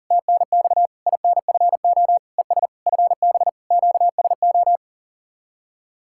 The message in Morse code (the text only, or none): TNX INFO ES FB QSO